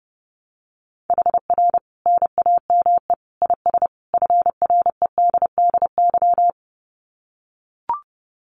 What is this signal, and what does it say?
Morse code, heard as a keyed tone, about 30 wpm, 700 Hz. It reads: HR NAME IS FREDDY
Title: Common QSO Phrases 30wpm